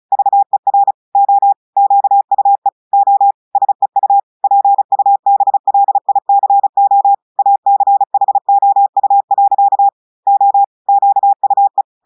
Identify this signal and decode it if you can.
Morse code, heard as a keyed tone, about 35 wpm, 800 Hz. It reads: VER O QUE O SEU PUBLICO ACHOU. O QUE
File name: audios / portugais / portugais texte en clair 2 vitesse 35 mots minutes